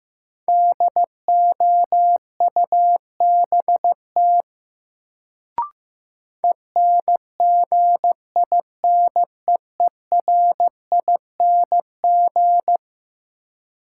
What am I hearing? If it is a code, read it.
Morse code, heard as a keyed tone, about 15 wpm, 700 Hz. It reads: DOUBT ENGINEERING
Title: Sentences from Top 1500 Words - New Word Drill - 15wpm